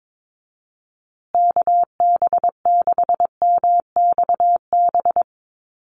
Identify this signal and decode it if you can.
Morse code, heard as a keyed tone, about 22 wpm, 700 Hz. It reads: KB6MXB
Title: USA Call Signs Repeated 2x 22wpm